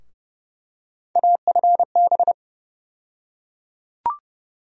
Morse code, heard as a keyed tone, about 30 wpm, 700 Hz. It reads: AFB
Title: USA Call Sign Postfixes - 3 Letters 30wpm